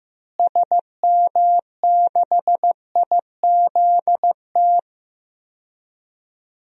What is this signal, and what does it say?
Morse code, heard as a keyed tone, about 16 wpm, 700 Hz. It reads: SM6IZT
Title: DXCC-100 Call Signs - 6 Characters - Repeated 3x - 15wpm